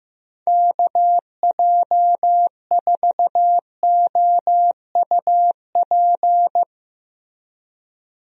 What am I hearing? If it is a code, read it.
Morse code, heard as a keyed tone, about 15 wpm, 700 Hz. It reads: KJ4OUP
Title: USA Call Signs - 6 Characters - Repeated 3x - 15wpm